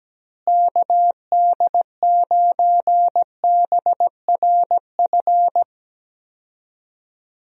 Morse code, heard as a keyed tone, about 17 wpm, 700 Hz. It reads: KD9BRF
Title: USA Call Signs - 6 Characters - Repeated 2x - 17wpm